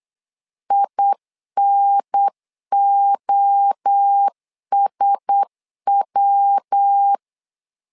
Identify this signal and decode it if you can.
Morse code, heard as a keyed tone, about 8 wpm, 800 Hz. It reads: INOSW
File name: LEZIONI / 4 Q1S 8wpm 2710